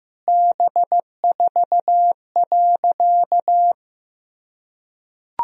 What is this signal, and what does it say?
Morse code, heard as a keyed tone, about 15 wpm, 700 Hz. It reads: B4.
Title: Sentences from Top 400 Words and Abbreviations 15wpm